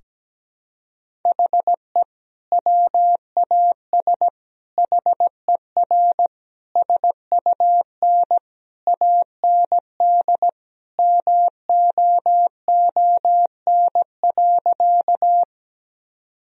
Morse code, heard as a keyed tone, about 17 wpm, 700 Hz. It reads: HE WAS HER SUN AND MOON.
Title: Sentences from Top 1000 Words - Review Part 3 - 17wpm